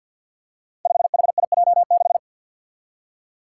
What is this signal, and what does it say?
Morse code, heard as a keyed tone, about 49 wpm, 700 Hz. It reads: 5HIJ6
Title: Five Challenging Letter-Number Combinations 50wpm